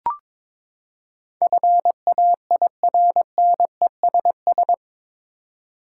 Morse code, heard as a keyed tone, about 22 wpm, 700 Hz. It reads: FAIRNESS